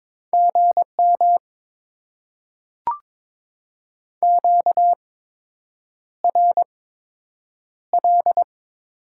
Morse code, heard as a keyed tone, about 23 wpm, spaced as for 8 wpm, 700 Hz. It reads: GM QRL
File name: Top 20 QSO Elements - 8x Character Spacing - 22wpm